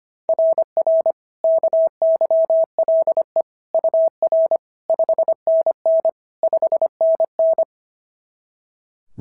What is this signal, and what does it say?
Morse code, heard as a keyed tone, about 25 wpm, 690 Hz. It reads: RR KYLE UR 5NN 5NN
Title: SOTA QSO Practice 25wpm